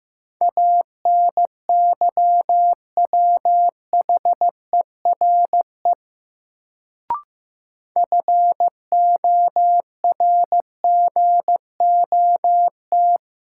Morse code, heard as a keyed tone, about 15 wpm, 700 Hz. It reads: ANYWHERE FORGOT
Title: Sentences from Top 1400 Words - New Word Drill Alt - 15wpm